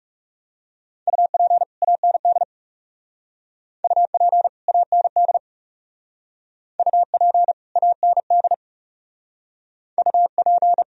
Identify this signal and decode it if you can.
Morse code, heard as a keyed tone, about 39 wpm, 700 Hz. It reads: UP AND UP AND UP AND UP